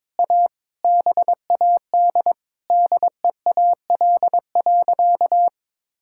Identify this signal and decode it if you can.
Morse code, heard as a keyed tone, about 22 wpm, 700 Hz. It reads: A BAD DEAL.
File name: Sentences with 9-Word Phrases - 22wpm